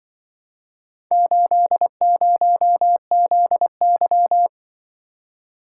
Morse code, heard as a keyed tone, about 24 wpm, 700 Hz. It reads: 80ZY
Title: Speed-Racing - Four Letter-Number Combinations - 20wpm